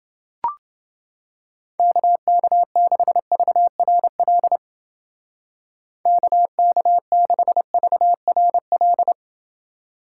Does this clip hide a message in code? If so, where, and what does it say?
Yes: Morse code from 1.8 to 9.1 s, heard as a keyed tone, about 28 wpm, 700 Hz. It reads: KK6VRL KK6VRL